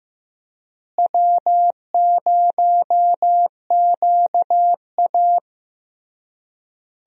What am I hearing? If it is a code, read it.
Morse code, heard as a keyed tone, about 15 wpm, 700 Hz. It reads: W0QA
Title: USA Call Signs - 4 Characters - Repeated 3x - 15wpm